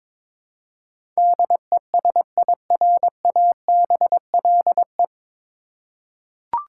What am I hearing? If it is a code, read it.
Morse code, heard as a keyed tone, about 22 wpm, 690 Hz. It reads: DESIRABLE
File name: 9 Letter Words - First 3 Letters Given - 22wpm